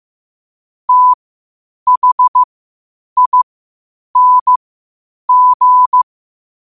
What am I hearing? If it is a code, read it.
Morse code, heard as a keyed tone, about 15 wpm, spaced as for 8 wpm, 1000 Hz. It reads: THING
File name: qotdfiles / 8wpm